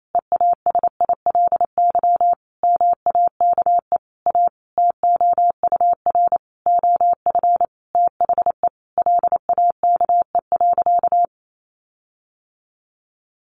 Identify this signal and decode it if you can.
Morse code, heard as a keyed tone, about 28 wpm, 700 Hz. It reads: EASILY MAKE A TOUR OF THE LAKE.
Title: Sentences from Top 1400 Words 28wpm